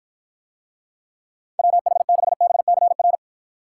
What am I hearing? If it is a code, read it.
Morse code, heard as a keyed tone, about 53 wpm, 700 Hz. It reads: WH6BCR